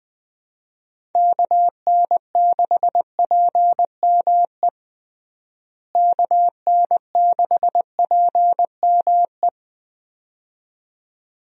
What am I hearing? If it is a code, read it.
Morse code, heard as a keyed tone, about 19 wpm, 700 Hz. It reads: KN6PME KN6PME
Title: USA Call Signs - 6 Characters - Repeated 2x - 20wpm